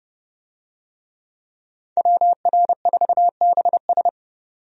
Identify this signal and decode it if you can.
Morse code, heard as a keyed tone, about 31 wpm, 700 Hz. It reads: WR4BS
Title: USA Call Signs - 5 Characters - 30wpm